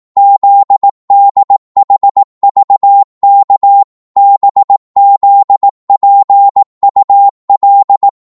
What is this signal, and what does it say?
Morse code, heard as a keyed tone, about 18 wpm, 800 Hz. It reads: ZDHVK BZPUL